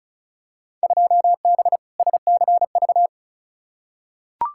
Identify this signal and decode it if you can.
Morse code, heard as a keyed tone, about 36 wpm, 700 Hz. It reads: 2B SCV
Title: Field Day Exchanges Encore - 35wpm